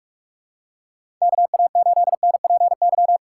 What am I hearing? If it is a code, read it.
Morse code, heard as a keyed tone, about 44 wpm, 700 Hz. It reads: KA8NPY
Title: USA Call Signs - 6 Characters - 6x Character Spacing - 45wpm